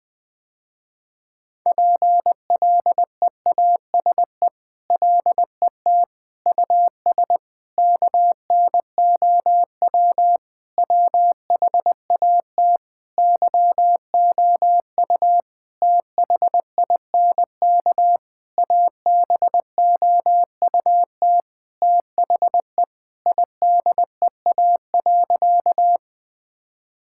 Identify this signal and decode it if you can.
Morse code, heard as a keyed tone, about 21 wpm, 700 Hz. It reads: PLEASE LET US KNOW WHAT YOU THINK ABOUT THE IDEA.
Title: Sentences with 6-Word Phrases - Spoken Beginning - 20wpm